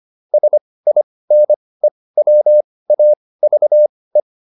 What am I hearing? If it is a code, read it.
Morse code, heard as a keyed tone, about 25 wpm, spaced as for 18 wpm, 600 Hz. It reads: SINE WAVE